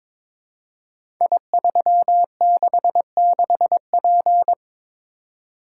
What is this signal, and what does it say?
Morse code, heard as a keyed tone, about 22 wpm, 700 Hz. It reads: I366P
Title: Code Groups - 5 Letter:Numbers - 22wpm